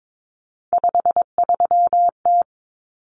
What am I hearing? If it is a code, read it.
Morse code, heard as a keyed tone, about 22 wpm, 700 Hz. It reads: 53T
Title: Common Cut Numbers - 3-Digit 22wpm